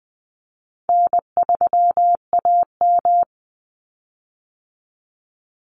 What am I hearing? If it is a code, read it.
Morse code, heard as a keyed tone, about 19 wpm, 700 Hz. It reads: N3AM
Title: Call Signs - 20wpm